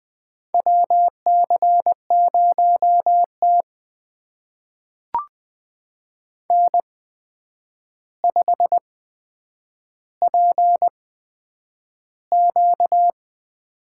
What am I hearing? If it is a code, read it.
Morse code, heard as a keyed tone, about 20 wpm, spaced as for 9 wpm, 700 Hz. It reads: WC0T N5PQ
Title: USA Call Signs - 4 Characters - 8x Character Spacing - 20wpm